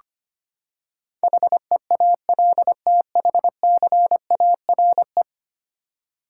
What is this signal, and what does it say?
Morse code, heard as a keyed tone, about 25 wpm, 700 Hz. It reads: HEALTHCARE
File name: Ten Letter Words - 25wpm